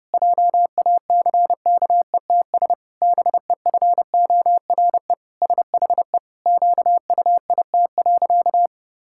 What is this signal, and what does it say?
Morse code, heard as a keyed tone, about 30 wpm, 700 Hz. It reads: JACKETS BEFORE SHE QUIT.
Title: Pangram Challenge v2 - 30wpm